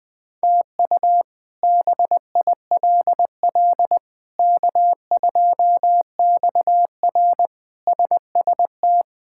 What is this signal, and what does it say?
Morse code, heard as a keyed tone, about 20 wpm, 700 Hz. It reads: TU BILL K2XR SST
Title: K1USN SST QSO Practice - 20wpm